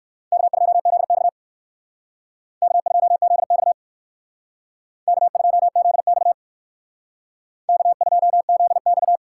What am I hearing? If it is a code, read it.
Morse code, heard as a keyed tone, about 57 wpm, 700 Hz. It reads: K2ZX K2ZX K2ZX K2ZX